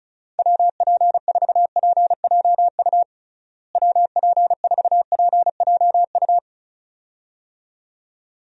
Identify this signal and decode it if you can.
Morse code, heard as a keyed tone, about 36 wpm, 700 Hz. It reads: WP4PJU WP4PJU
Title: USA Call Signs - 6 Characters - Repeated 2x - 35wpm